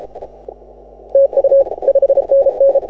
Morse code, heard as a keyed tone, about 36 wpm, 550 Hz. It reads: TU 5NN
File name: AUDIO2018 / HA6OI CW